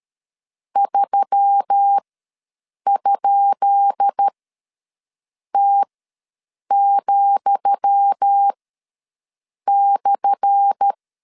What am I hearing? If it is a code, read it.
Morse code, heard as a keyed tone, about 13 wpm, spaced as for 6 wpm, 800 Hz. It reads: 3?T,/
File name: LEZIONI / 10 SIGNS / 10D COMA ws 12wpm 3618